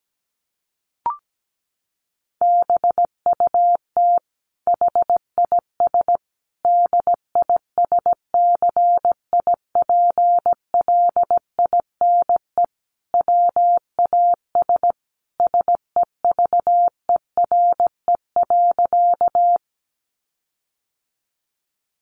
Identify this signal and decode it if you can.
Morse code, heard as a keyed tone, about 17 wpm, 700 Hz. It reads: BUT HIS DISCIPLINE WAS SEVERE.
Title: Sentences from Top 1400 Words 17wpm